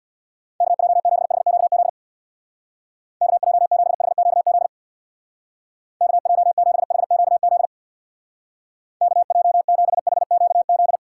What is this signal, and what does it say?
Morse code, heard as a keyed tone, about 64 wpm, 700 Hz. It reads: KJ7HQZ KJ7HQZ KJ7HQZ KJ7HQZ